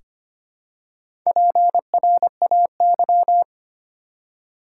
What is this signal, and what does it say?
Morse code, heard as a keyed tone, about 25 wpm, 700 Hz. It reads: PRAY
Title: Four Letter Words - 25wpm